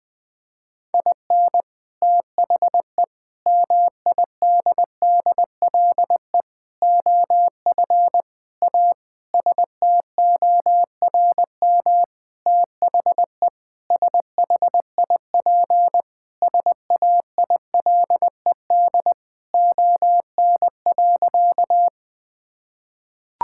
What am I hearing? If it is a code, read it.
Morse code, heard as a keyed tone, about 20 wpm, 700 Hz. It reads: IN THE MIDDLE OF A STORM THE SHIP SAILED ON.
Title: Sentences with 5-Word Phrases - 20wpm